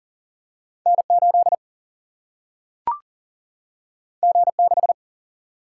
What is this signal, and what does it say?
Morse code, heard as a keyed tone, about 40 wpm, 700 Hz. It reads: N 8 G 6
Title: Single-Letter Single-Number 40wpm